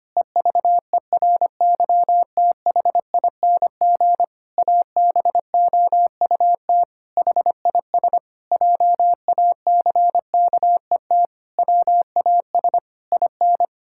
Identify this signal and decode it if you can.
Morse code, heard as a keyed tone, about 25 wpm, 700 Hz. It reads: EVERYTHING ABOUT HIS JACKET WAS IN
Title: Pangram Challenge v2 - 25wpm